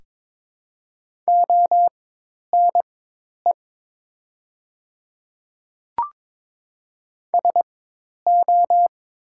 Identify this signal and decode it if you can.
Morse code, heard as a keyed tone, about 22 wpm, spaced as for 10 wpm, 700 Hz. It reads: ONE SO
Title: 4x Character Spacing - Top 100 Words - 22wpm